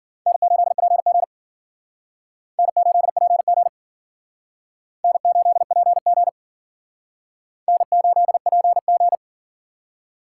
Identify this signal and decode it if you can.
Morse code, heard as a keyed tone, about 52 wpm, 700 Hz. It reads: N8PG N8PG N8PG N8PG